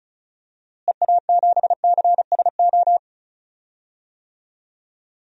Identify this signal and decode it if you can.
Morse code, heard as a keyed tone, about 34 wpm, 700 Hz. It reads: EA7CSO
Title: DXCC-100 Call Signs - 6 Characters - Repeated 3x - 35wpm